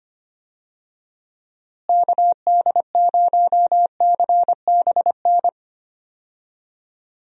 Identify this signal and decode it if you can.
Morse code, heard as a keyed tone, about 25 wpm, 700 Hz. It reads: KD0CBN
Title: USA Call Signs - 6 Characters - Repeated 3x - 25wpm